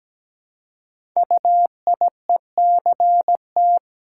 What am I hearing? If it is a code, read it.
Morse code, heard as a keyed tone, about 17 wpm, 700 Hz. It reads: UIECT
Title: Code Groups - 5 Letters - 17wpm